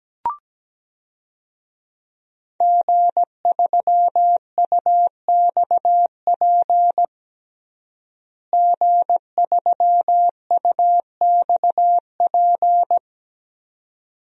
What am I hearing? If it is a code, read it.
Morse code, heard as a keyed tone, about 17 wpm, 700 Hz. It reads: G3UXP G3UXP
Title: DXCC-100 Call Signs - 5 Characters - Repeated 2x - 17wpm